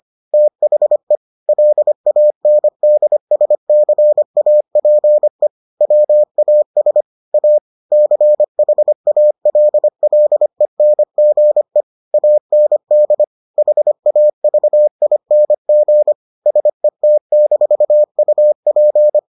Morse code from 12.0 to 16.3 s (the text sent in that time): AND HAVING